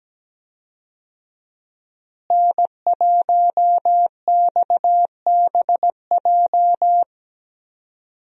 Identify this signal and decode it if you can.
Morse code, heard as a keyed tone, about 17 wpm, 700 Hz. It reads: N1XBJ
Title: USA Call Signs - 5 Characters - Repeated 2x - 17wpm